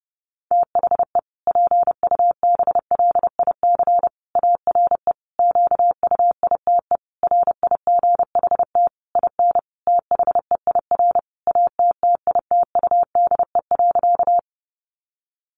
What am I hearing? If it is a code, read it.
Morse code, heard as a keyed tone, about 30 wpm, 700 Hz. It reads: THE PUBLIC ARE QUITE RIGHT IN THEIR ATTITUDE.
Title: Sentences from Top 1300 Words 30wpm